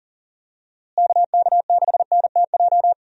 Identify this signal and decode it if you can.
Morse code, heard as a keyed tone, about 38 wpm, 700 Hz. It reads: KK6NTJ
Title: USA Call Signs - 6 Characters - Repeated 3x - 40wpm